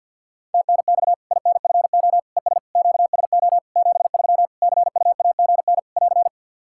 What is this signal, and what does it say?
Morse code, heard as a keyed tone, about 53 wpm, 700 Hz. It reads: TNX INFO ES QSO 73 CUAGN <AR>
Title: Practice QSO - Exchange 4 - Ending - 50wpm